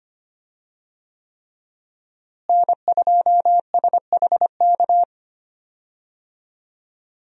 Morse code, heard as a keyed tone, about 25 wpm, 700 Hz. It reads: N2SHK
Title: USA Call Signs - 5 Characters - Repeated 3x - 25wpm